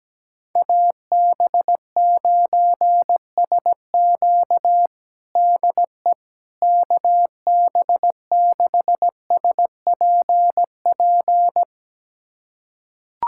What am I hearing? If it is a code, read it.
Morse code, heard as a keyed tone, about 17 wpm, 700 Hz. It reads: AB9SQ DE KB6SPP
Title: USA Callsign DE Callsign 17wpm